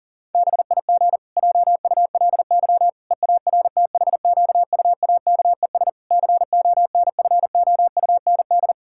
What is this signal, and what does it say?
Morse code, heard as a keyed tone, about 37 wpm, 700 Hz. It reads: BIG JULY EARTHQUAKES CONFOUND
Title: Pangram Challenge v2 - 40wpm